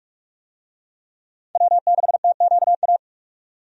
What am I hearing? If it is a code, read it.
Morse code, heard as a keyed tone, about 45 wpm, 700 Hz. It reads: W6TQA
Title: USA Call Signs - 5 Characters - 45wpm